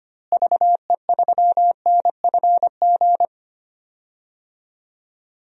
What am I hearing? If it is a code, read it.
Morse code, heard as a keyed tone, about 25 wpm, 700 Hz. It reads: VE3NFG
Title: DXCC-100 Call Signs - 6 Characters - Repeated 3x - 25wpm